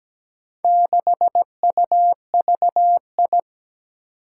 Morse code, heard as a keyed tone, about 17 wpm, 700 Hz. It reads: 6UVI
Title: Four Challenging Letter-Number Combinations 17wpm